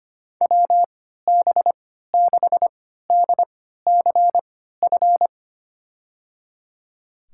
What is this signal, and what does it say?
Morse code, heard as a keyed tone, about 25 wpm, spaced as for 14 wpm, 700 Hz. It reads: WB6DCF
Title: USA Call Signs - 6 Characters - 3x Character Spacing - 25wpm